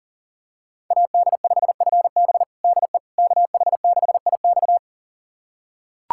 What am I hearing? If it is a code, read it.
Morse code, heard as a keyed tone, about 40 wpm, 700 Hz. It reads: AD5FB DE KH6IX
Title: USA Callsign DE Callsign - 5 Characters - 40wpm